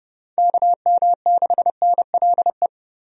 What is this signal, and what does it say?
Morse code, heard as a keyed tone, about 30 wpm, 700 Hz. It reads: KM6NLE
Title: USA Call Signs - 6 Characters - Repeated 3x - 30wpm